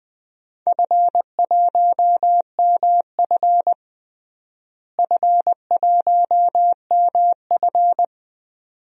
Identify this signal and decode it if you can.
Morse code, heard as a keyed tone, about 20 wpm, 700 Hz. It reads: F1MF F1MF
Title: DXCC-100 Call Signs - 4 Characters - Repeated 2x - 20wpm